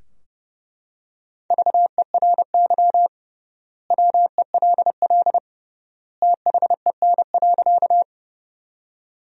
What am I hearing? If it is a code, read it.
Morse code, heard as a keyed tone, about 32 wpm, 700 Hz. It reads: VERY WELL THEN.
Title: 3x Word-Spacing - Sentences from Top 200 Words - 30wpm